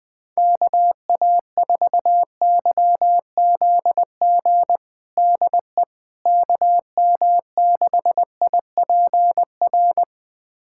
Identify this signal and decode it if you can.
Morse code, heard as a keyed tone, about 21 wpm, 700 Hz. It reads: KA4YZG DE KM6IPR
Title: USA Callsign DE Callsign - 6 Characters - 20wpm